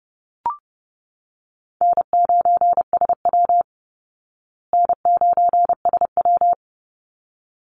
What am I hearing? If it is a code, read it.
Morse code, heard as a keyed tone, about 30 wpm, 700 Hz. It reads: N9SW N9SW
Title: Easy Call Signs Repeated 2x - 30wpm